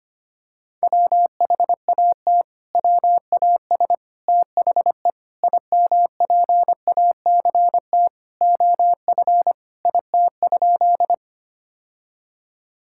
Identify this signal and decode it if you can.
Morse code, heard as a keyed tone, about 25 wpm, 700 Hz. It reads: WHAT WAS THE IMPACT OF IT?
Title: Sentences from Top 1000 Words - Review Part 6 - 25wpm